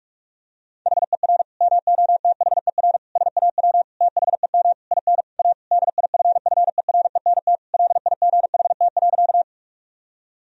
Morse code, heard as a keyed tone, about 41 wpm, 700 Hz. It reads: HER MOTHER SAW THEM IN A DIFFERENT LIGHT.
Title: Sentences from Top 1000 Words - Review Part 3 - 45wpm